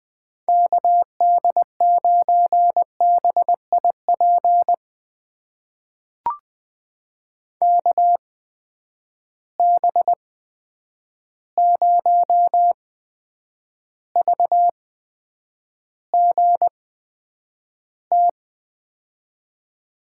Morse code, heard as a keyed tone, about 20 wpm, spaced as for 9 wpm, 700 Hz. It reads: KD9BIP KB0VGT
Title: USA Call Signs - 6 Characters - 8x Character Spacing - 20wpm